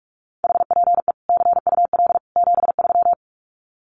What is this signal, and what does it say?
Morse code, heard as a keyed tone, about 44 wpm, 700 Hz. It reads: HPE CUL 73
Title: Common QSO Phrases 45wpm